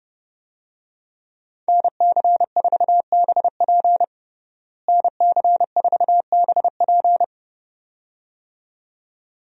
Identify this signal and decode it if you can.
Morse code, heard as a keyed tone, about 30 wpm, 700 Hz. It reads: NC4BP NC4BP
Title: USA Call Signs - 5 Characters - Repeated 2x - 30wpm